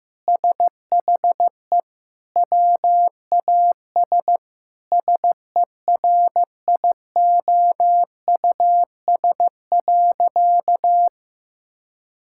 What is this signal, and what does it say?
Morse code, heard as a keyed tone, about 16 wpm, 700 Hz. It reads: SHE WAS SERIOUS.
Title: Sentences from Top 1600 Words 15wpm